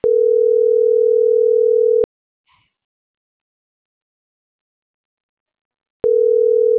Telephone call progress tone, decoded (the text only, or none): ringback tone